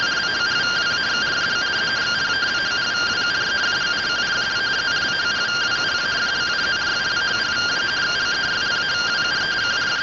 Two-tone teleprinter signal, about 170 Hz shift.